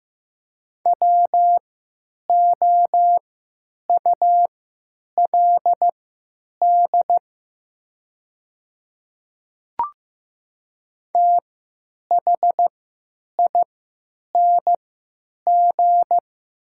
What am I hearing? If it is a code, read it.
Morse code, heard as a keyed tone, about 15 wpm, spaced as for 9 wpm, 690 Hz. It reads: WOULD THING
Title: 3x Character Spacing - Top 100 Words - 15wpm